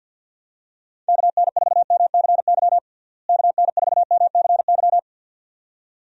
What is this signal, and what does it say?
Morse code, heard as a keyed tone, about 49 wpm, 700 Hz. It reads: KN4MCY KN4MCY
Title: USA Call Signs - 6 Characters - Repeated 2x - 50wpm